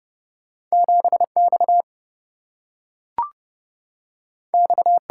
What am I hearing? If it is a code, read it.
Morse code, heard as a keyed tone, about 30 wpm, 700 Hz. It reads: 7X X
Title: Single-Letter Single-Number 30wpm